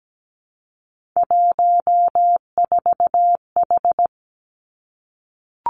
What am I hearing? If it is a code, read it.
Morse code, heard as a keyed tone, about 17 wpm, 700 Hz. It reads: 14H
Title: Three Challenging Letter-Number Combinations 17wpm